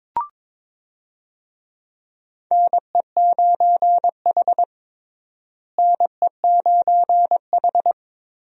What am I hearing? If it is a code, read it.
Morse code, heard as a keyed tone, about 22 wpm, 700 Hz. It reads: NE9H NE9H
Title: USA Call Signs - 4 Characters - Repeated 3x - 22wpm